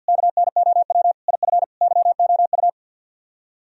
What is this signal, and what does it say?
Morse code, heard as a keyed tone, about 62 wpm, 700 Hz. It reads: KNOW IF YOU
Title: Speed-Racing - Sets of 5 Words 50wpm